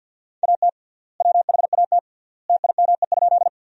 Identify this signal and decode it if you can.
Morse code, heard as a keyed tone, about 52 wpm, 700 Hz. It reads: AT WHAT TIME?
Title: 3x Word-Spacing - Sentences from Top 100 Words - 50wpm